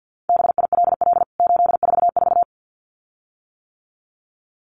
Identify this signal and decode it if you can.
Morse code, heard as a keyed tone, about 50 wpm, 700 Hz. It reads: BILL 844
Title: CWT Exchange 50wpm